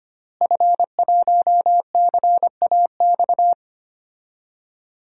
Morse code, heard as a keyed tone, about 25 wpm, 700 Hz. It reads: F1CAX
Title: DXCC-100 Call Signs - 5 Characters - Repeated 2x - 25wpm